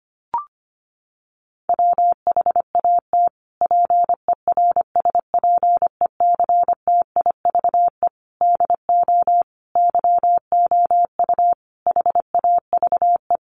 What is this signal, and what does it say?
Morse code, heard as a keyed tone, about 25 wpm, 700 Hz. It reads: WHAT PERSPECTIVE DO YOU HAVE
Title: Sentences from Top 1200 Words 25wpm